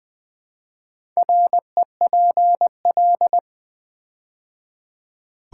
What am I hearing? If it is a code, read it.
Morse code, heard as a keyed tone, about 20 wpm, 700 Hz. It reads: REPL
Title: Build A Word - 4 Letters At A Time - 11-Letter Words - 20wpm